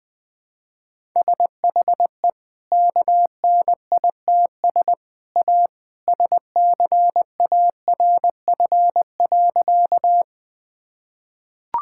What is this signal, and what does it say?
Morse code, heard as a keyed tone, about 20 wpm, 700 Hz. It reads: SHE KNITS A SCARF.
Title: Easy Sentences with Hints - 20wpm